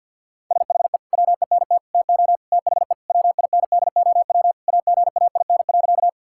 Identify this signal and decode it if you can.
Morse code, heard as a keyed tone, about 50 wpm, 700 Hz. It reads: SHE WENT TO THE WINDOW AGAIN.
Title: Sentences from Top 1600 Words 50wpm